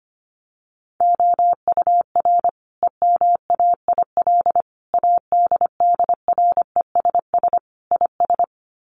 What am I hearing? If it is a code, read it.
Morse code, heard as a keyed tone, about 25 wpm, 700 Hz. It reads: OUR EMAIL ADDRESS IS
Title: Sets of 5 Words Encore2 25wpm